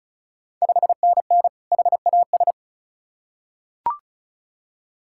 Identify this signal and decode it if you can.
Morse code, heard as a keyed tone, about 36 wpm, 690 Hz. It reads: 5NN HAS
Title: Ontario QSO Party Exchange 35wpm